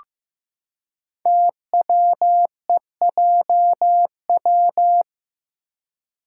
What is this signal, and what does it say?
Morse code, heard as a keyed tone, about 15 wpm, 700 Hz. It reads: TWEJW
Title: Code Groups - 5 Letters - 15wpm